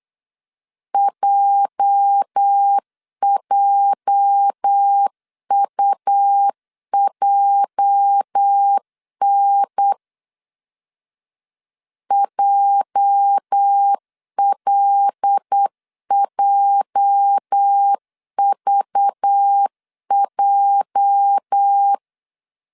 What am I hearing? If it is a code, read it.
Morse code, heard as a keyed tone, about 8 wpm, 800 Hz. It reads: JJUJN JLJVJ
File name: LEZIONI / 8 YJ45 / 8 YJ 8wpm 3313